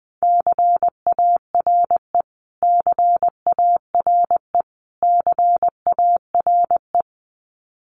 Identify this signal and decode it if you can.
Morse code, heard as a keyed tone, about 20 wpm, 700 Hz. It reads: CARE CARE CARE
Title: Top 500 Words Repeated 20wpm